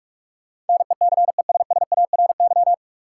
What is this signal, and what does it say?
Morse code, heard as a keyed tone, about 43 wpm, 700 Hz. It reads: NECESSARY
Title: Sentences from Top 800 Words - New Word Drill - 45wpm